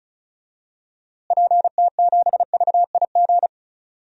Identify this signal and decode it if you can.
Morse code, heard as a keyed tone, about 36 wpm, 700 Hz. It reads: PT7VIG
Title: DXCC-100 Call Signs - 6 Characters - Repeated 3x - 35wpm